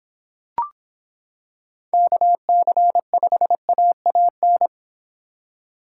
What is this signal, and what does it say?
Morse code, heard as a keyed tone, about 26 wpm, 700 Hz. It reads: KC5AAN